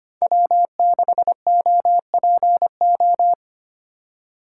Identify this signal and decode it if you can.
Morse code, heard as a keyed tone, about 25 wpm, 700 Hz. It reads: W6OPO
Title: K1USN SST QSO Practice - 25wpm